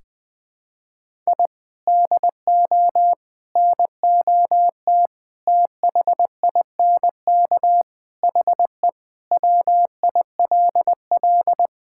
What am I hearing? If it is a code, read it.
Morse code, heard as a keyed tone, about 20 wpm, 700 Hz. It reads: I DO NOT THINK HE WILL
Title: Sentences with 5-Word Phrases - 20wpm